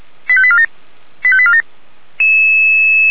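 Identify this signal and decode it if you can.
Five-tone selective call (ZVEI), heard as quick runs of five tones, tones of about 70 ms. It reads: 85847 85847
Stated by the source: ZVEI five-tone selective call received on SDR.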